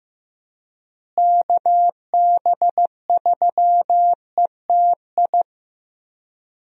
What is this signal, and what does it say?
Morse code, heard as a keyed tone, about 16 wpm, 700 Hz. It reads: KB3ETI
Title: USA Call Signs - 6 Characters - Repeated 3x - 15wpm